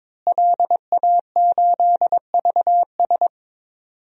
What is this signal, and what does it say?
Morse code, heard as a keyed tone, about 21 wpm, 700 Hz. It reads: LA8VS
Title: DXCC-100 Call Signs - 5 Characters - Repeated 3x - 22wpm